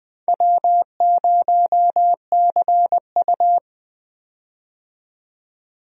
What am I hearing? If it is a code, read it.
Morse code, heard as a keyed tone, about 21 wpm, 700 Hz. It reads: W0CU
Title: USA Call Signs - 4 Characters - 20wpm